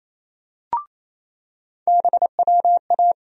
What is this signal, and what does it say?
Morse code, heard as a keyed tone, about 28 wpm, 700 Hz. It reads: BWA
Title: USA Call Sign Postfixes - 3 Letters 28wpm